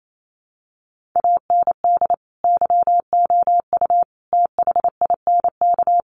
Morse code, heard as a keyed tone, about 28 wpm, 700 Hz. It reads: AND YOU THINK
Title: Sentences from Top 1500 Words 28wpm